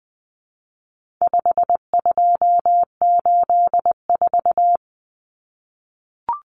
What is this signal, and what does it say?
Morse code, heard as a keyed tone, about 20 wpm, 700 Hz. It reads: 5284
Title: Common Cut Numbers - 4-Digit 20wpm